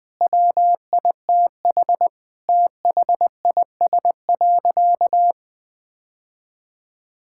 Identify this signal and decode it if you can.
Morse code, heard as a keyed tone, about 20 wpm, 700 Hz. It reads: WITH THIS.
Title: Sentences from Top 1000 Words - Review Part 6 - 20wpm